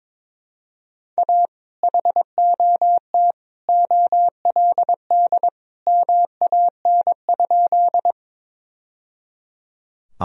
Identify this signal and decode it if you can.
Morse code, heard as a keyed tone, about 22 wpm, 700 Hz. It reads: A HOT OLD MAN?
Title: Sentences from Top 1000 Words - Review Part 3 - 22wpm